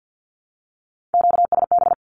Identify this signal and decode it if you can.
Morse code, heard as a keyed tone, about 48 wpm, 700 Hz. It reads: QSB
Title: Q-Codes Repeated 3x 50wpm